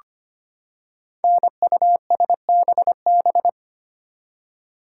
Morse code, heard as a keyed tone, about 25 wpm, 700 Hz. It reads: NUSBB
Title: Code Groups - 5 Letters - 25wpm